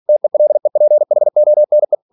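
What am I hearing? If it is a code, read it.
Morse code, heard as a keyed tone, about 46 wpm, 600 Hz. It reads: TELEPHONE
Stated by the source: Practice text from an English novel sent at 47 WPM.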